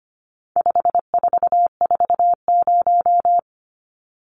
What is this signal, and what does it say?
Morse code, heard as a keyed tone, about 25 wpm, 700 Hz. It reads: 5440
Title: Numbers - 4-Digit 25wpm